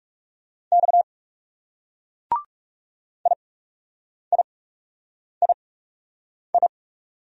Morse code, heard as a keyed tone, about 36 wpm, spaced as for 8 wpm, 700 Hz. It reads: X IIII
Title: Speed-Racing Single Letters 30wpm